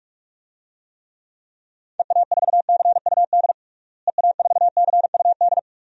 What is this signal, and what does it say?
Morse code, heard as a keyed tone, about 46 wpm, 700 Hz. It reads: EA4CUD EA4CUD
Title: DXCC-100 Call Signs - 6 Characters - Repeated 2x - 45wpm